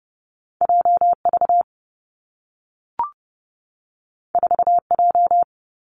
Morse code, heard as a keyed tone, about 30 wpm, 700 Hz. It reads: J V 4 J
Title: Two Challenging Letter-Number Combinations 30wpm